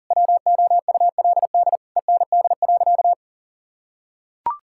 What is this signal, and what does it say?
Morse code, heard as a keyed tone, about 37 wpm, 700 Hz. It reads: WOULD END.
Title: Sentences from Top 1600 Words 40wpm